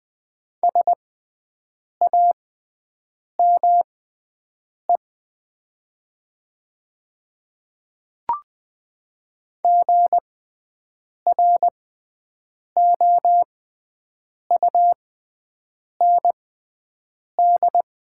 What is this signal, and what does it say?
Morse code, heard as a keyed tone, about 20 wpm, spaced as for 7 wpm, 700 Hz. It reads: SAME GROUND
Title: 6x Character Spacing - Top 500 Words - 20wpm